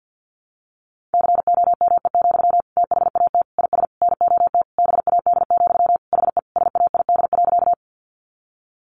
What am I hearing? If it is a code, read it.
Morse code, heard as a keyed tone, about 50 wpm, 700 Hz. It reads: COME, THAT IS NOT BAD, HE SAID.
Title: Sentences from Top 700 Words 50wpm